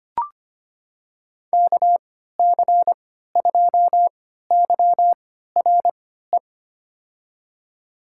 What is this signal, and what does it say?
Morse code, heard as a keyed tone, about 25 wpm, spaced as for 14 wpm, 700 Hz. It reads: KC2YRE
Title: USA Call Signs - 6 Characters - 3x Character Spacing - 25wpm